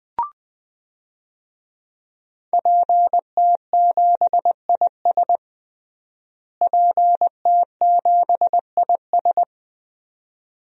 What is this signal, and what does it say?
Morse code, heard as a keyed tone, about 20 wpm, 700 Hz. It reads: PT7IS PT7IS
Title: DXCC-100 Call Signs - 5 Characters - Repeated 2x - 20wpm